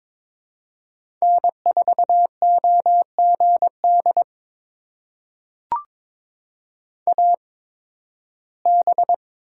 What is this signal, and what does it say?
Morse code, heard as a keyed tone, about 22 wpm, spaced as for 14 wpm, 700 Hz. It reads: N4OGD AB
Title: USA Call Signs - 8x Character Spacing - 22wpm